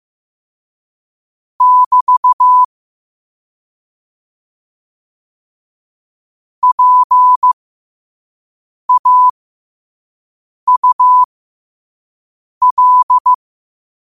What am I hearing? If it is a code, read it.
Morse code, heard as a keyed tone, about 15 wpm, spaced as for 5 wpm, 1000 Hz. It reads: <BT> PAUL
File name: 5wpm